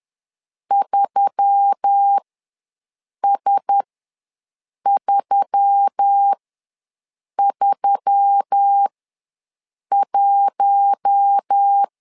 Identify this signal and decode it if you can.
Morse code, heard as a keyed tone, about 10 wpm, spaced as for 6 wpm, 800 Hz. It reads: 3S331
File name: LEZIONI / 4 Q1S3 ws 10wpm 2437